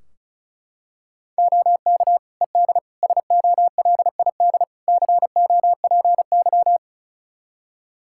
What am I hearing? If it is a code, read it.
Morse code, heard as a keyed tone, about 36 wpm, 700 Hz. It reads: OK ED SOLID COPY
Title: Practice QSO - Exchange 3 - More Info - 35wpm